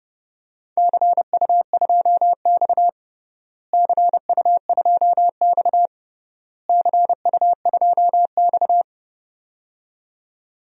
Morse code, heard as a keyed tone, about 30 wpm, 700 Hz. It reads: CU2X CU2X CU2X
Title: DXCC-100 Call Signs - 4 Characters - Repeated 3x - 30wpm